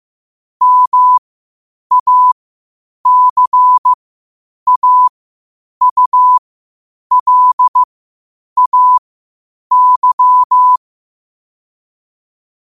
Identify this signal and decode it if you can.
Morse code, heard as a keyed tone, about 15 wpm, spaced as for 8 wpm, 1000 Hz. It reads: MACAULAY
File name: qotdfiles / 8wpm